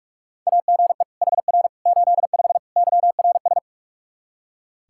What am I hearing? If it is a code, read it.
Morse code, heard as a keyed tone, about 41 wpm, 700 Hz. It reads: AGE HR 85 YRS
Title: Practice QSO - Exchange 3 - More Info - 45wpm